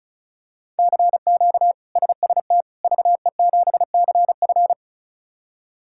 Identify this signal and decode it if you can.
Morse code, heard as a keyed tone, about 36 wpm, 700 Hz. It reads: CQ SST VE7CF
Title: K1USN SST QSO Practice - 35wpm